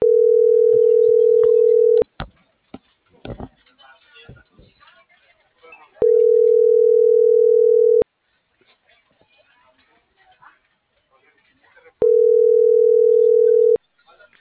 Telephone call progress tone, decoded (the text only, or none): ringback tone